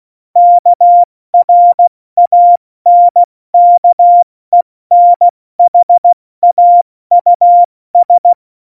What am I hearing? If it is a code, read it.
Morse code, heard as a keyed tone, about 16 wpm, spaced as for 14 wpm, 700 Hz. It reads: KRANKENHAUS
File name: Morsekurs / Witze / Witz 20. Oktober 2020, 70 F80